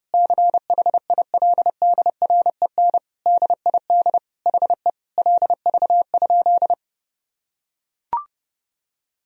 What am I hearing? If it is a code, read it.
Morse code, heard as a keyed tone, about 30 wpm, 700 Hz. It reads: CHILDREN DID HE LV?
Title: Sentences from Top 400 Words and Abbreviations 30wpm